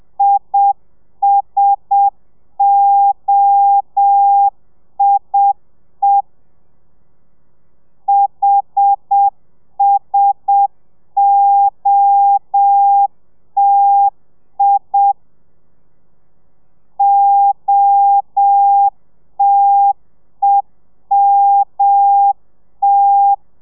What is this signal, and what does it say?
Morse code, heard as a keyed tone, about 7 wpm, 800 Hz. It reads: ISOIE HSOTI OTEMT
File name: telegrafia / i2tfj / Lez1 30